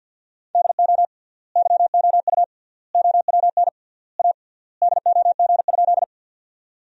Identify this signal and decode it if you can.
Morse code, heard as a keyed tone, about 50 wpm, 700 Hz. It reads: DO YOU OWN A DOG?
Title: 3x Word-Spacing - Sentences from Top 400 Words - 50wpm